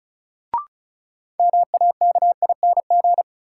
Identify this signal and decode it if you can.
Morse code, heard as a keyed tone, about 34 wpm, 700 Hz. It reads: MAKING
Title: Sentences from Top 1000 Words - Review Part 6 - New Word Drill - 35wpm